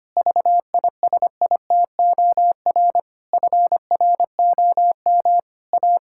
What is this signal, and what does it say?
Morse code, heard as a keyed tone, about 25 wpm, 700 Hz. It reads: VISITOR FROM A
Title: Pangram Challenge v2 - 25wpm